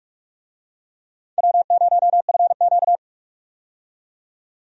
Morse code, heard as a keyed tone, about 46 wpm, 700 Hz. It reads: W0FQ
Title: USA Call Signs - 4 Characters - 45wpm